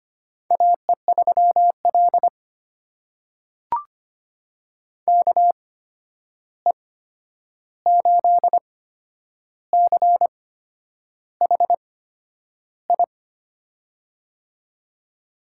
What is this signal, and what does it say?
Morse code, heard as a keyed tone, about 25 wpm, spaced as for 9 wpm, 700 Hz. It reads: AE3L KE8CHI
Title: USA Call Signs - 8x Character Spacing - 25wpm